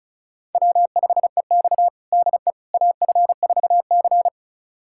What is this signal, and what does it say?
Morse code, heard as a keyed tone, about 36 wpm, 700 Hz. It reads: W5EX DE AF4C
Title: USA Callsign DE Callsign - 4 Characters - 35wpm